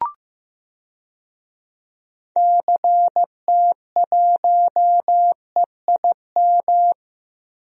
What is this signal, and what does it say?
Morse code, heard as a keyed tone, about 15 wpm, 700 Hz. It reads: CT1EIM
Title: DXCC-100 Call Signs - 6 Characters - Repeated 3x - 15wpm